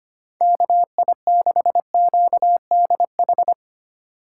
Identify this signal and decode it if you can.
Morse code, heard as a keyed tone, about 25 wpm, 700 Hz. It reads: KI6QDH
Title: K1USN SST QSO Practice - 25wpm